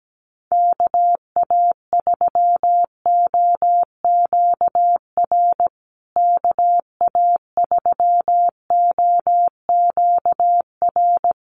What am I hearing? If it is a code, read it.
Morse code, heard as a keyed tone, about 17 wpm, 700 Hz. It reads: KA3OQR KA3OQR
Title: USA Call Signs Repeated 3x 17wpm